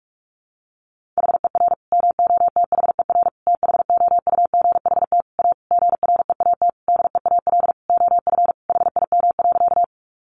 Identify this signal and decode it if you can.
Morse code, heard as a keyed tone, about 44 wpm, 700 Hz. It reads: HER MOTHER THOUGHT A GREAT DEAL OF HIM.
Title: Sentences from Top 800 Words 45wpm